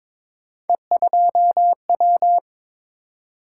Morse code, heard as a keyed tone, about 23 wpm, 700 Hz. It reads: E2W
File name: practice / 22wpm / Three Letter-Number Combinations - 22wpm